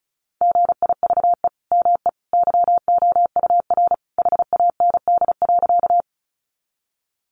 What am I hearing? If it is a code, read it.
Morse code, heard as a keyed tone, about 34 wpm, 700 Hz. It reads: GIVE ME YOUR HAND.
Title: Sentences from Top 200 Words 35wpm